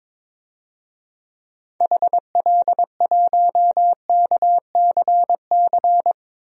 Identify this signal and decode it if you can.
Morse code, heard as a keyed tone, about 22 wpm, 700 Hz. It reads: HL1KCC
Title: DXCC-100 Call Signs - 6 Characters - Repeated 3x - 22wpm